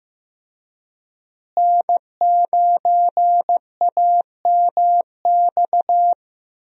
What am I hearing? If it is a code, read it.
Morse code, heard as a keyed tone, about 15 wpm, 700 Hz. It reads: N9AMX
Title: USA Call Signs - 5 Characters - Repeated 3x - 15wpm